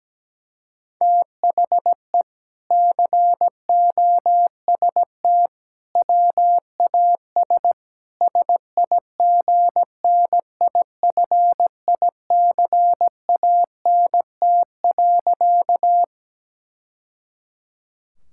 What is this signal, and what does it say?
Morse code, heard as a keyed tone, about 17 wpm, 700 Hz. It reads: THE COST WAS SIGNIFICANT.
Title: Sentences from Top 1000 Words - Review Part 6 - 17wpm